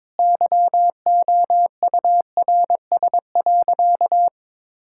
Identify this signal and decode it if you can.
Morse code, heard as a keyed tone, about 22 wpm, 700 Hz. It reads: YOURS.
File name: practice / 22wpm / Sentences with 8-Word Phrases - 22wpm